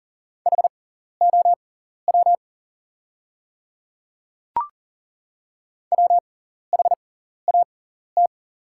Morse code, heard as a keyed tone, about 40 wpm, spaced as for 14 wpm, 700 Hz. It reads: HOW WHAT
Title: 6x Character Spacing - Top 100 Words - 40wpm